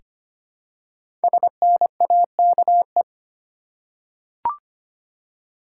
Morse code, heard as a keyed tone, about 25 wpm, 700 Hz. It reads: SNAKE
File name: 5 Letter Words - First 3 Letters Given - 25wpm